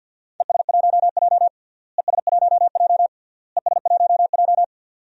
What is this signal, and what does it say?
Morse code, heard as a keyed tone, about 48 wpm, 700 Hz. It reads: ES1J ES1J ES1J
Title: DXCC-100 Call Signs - 4 Characters - Repeated 3x - 50wpm